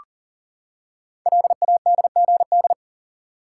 Morse code, heard as a keyed tone, about 42 wpm, 700 Hz. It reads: LADGD
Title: Code Groups - 5 Letters - 40wpm